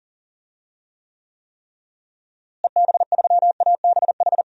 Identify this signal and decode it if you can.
Morse code, heard as a keyed tone, about 40 wpm, 700 Hz. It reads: EB3ABH
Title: DXCC-100 Call Signs - 6 Characters - Repeated 3x - 40wpm